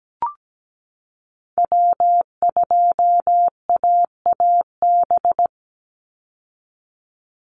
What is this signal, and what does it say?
Morse code, heard as a keyed tone, about 17 wpm, 700 Hz. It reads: W2AAB
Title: Call Signs Repeated 2x - 17wpm